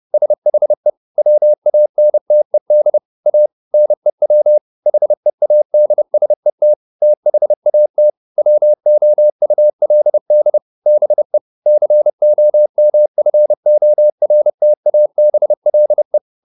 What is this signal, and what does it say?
Morse code, heard as a keyed tone, about 30 wpm, 600 Hz. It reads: SHE WANTED A NEW HEADSET THAT WOULD BE COMFORTABLE